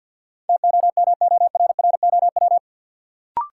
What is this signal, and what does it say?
Morse code, heard as a keyed tone, about 50 wpm, 700 Hz. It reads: TOMORROW
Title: Sentences from Top 1100 Words - New Word Drill - 50wpm